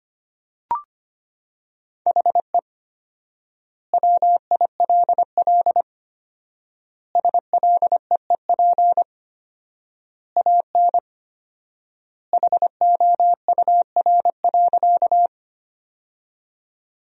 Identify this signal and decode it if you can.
Morse code, heard as a keyed tone, about 25 wpm, 700 Hz. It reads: HE WILL SLEEP AN HOUR.
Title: 4x Word-Spacing - Sentences from Top 500 Words - 25wpm